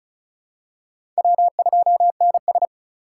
Morse code, heard as a keyed tone, about 36 wpm, 700 Hz. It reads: W2NS
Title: USA Call Signs - 4 Characters - 2x Character Spacing - 35wpm